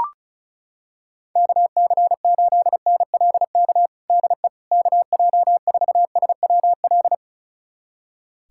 Morse code, heard as a keyed tone, about 34 wpm, 700 Hz. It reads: KC8NLK DE KJ4SWL
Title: USA Callsign DE Callsign 35wpm